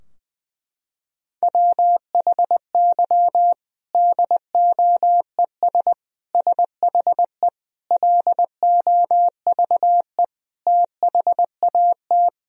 Morse code, heard as a keyed tone, about 20 wpm, 700 Hz. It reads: WHY DOES SHE LOVE THAT
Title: Sentences from Top 1000 Words - Review Part 4 - 20wpm